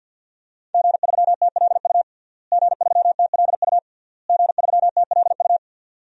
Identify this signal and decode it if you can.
Morse code, heard as a keyed tone, about 52 wpm, 700 Hz. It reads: G3TLU G3TLU G3TLU
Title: DXCC-100 Call Signs - 5 Characters - Repeated 3x - 50wpm